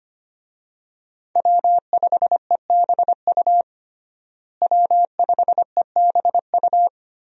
Morse code, heard as a keyed tone, about 25 wpm, 700 Hz. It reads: W5EBU W5EBU
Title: USA Call Signs - 5 Characters - Repeated 2x - 25wpm